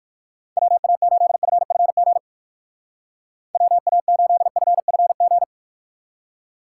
Morse code, heard as a keyed tone, about 57 wpm, 700 Hz. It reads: WA8FFG WA8FFG